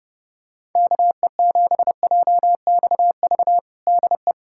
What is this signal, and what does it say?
Morse code, heard as a keyed tone, about 30 wpm, 700 Hz. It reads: KE7JXV DE
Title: USA Callsign DE Callsign 30wpm